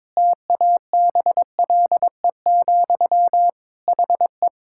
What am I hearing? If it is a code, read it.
Morse code, heard as a keyed tone, about 21 wpm, 700 Hz. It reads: TABLE, HE
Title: Sentences from Top 1000 Words - Review Part 6 - 22wpm